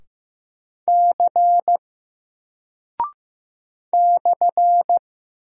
Morse code, heard as a keyed tone, about 15 wpm, 700 Hz. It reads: C /
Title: Single Character (Letter:Number:Punctuation:Prosign) Alt 15wpm